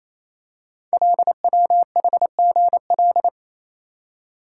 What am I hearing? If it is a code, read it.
Morse code, heard as a keyed tone, about 27 wpm, 700 Hz. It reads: LWHGL
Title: Code Groups - 5 Letter:Numbers - 28wpm